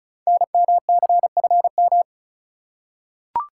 Morse code, heard as a keyed tone, about 36 wpm, 700 Hz. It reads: NMCFM
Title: Code Groups - 5 Letters - 35wpm